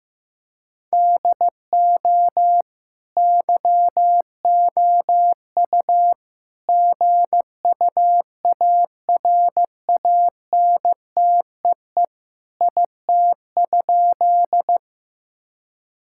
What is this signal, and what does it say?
Morse code, heard as a keyed tone, about 15 wpm, 700 Hz. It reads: DO YOU GUARANTEE IT?
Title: Sentences from Top 1600 Words 15wpm